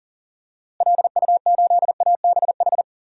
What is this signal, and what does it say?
Morse code, heard as a keyed tone, about 37 wpm, 700 Hz. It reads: LU8ABH
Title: DXCC-100 Call Signs - 6 Characters - Repeated 2x - 40wpm